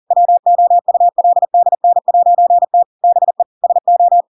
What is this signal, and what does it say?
Morse code, heard as a keyed tone, about 40 wpm, 700 Hz. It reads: WOULDN'T BE SO
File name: Pinocchio / Pinocchio ch5 40 40 ik1hge